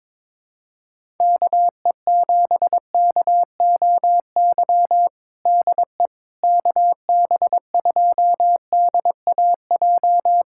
Morse code, heard as a keyed tone, about 22 wpm, 700 Hz. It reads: KE7KOY DE KB2DAJ
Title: USA Callsign DE Callsign - 6 Characters - 22wpm